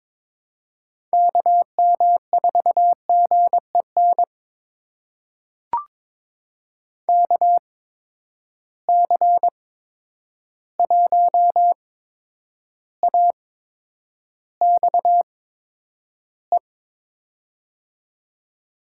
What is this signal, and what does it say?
Morse code, heard as a keyed tone, about 22 wpm, spaced as for 9 wpm, 700 Hz. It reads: KM4GEN KC1AXE
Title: USA Call Signs - 6 Characters - 8x Character Spacing - 22wpm